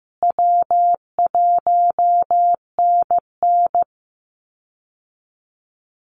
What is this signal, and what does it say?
Morse code, heard as a keyed tone, about 15 wpm, 700 Hz. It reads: W1NN
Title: Easy Call Signs Repeated 3x - 15wpm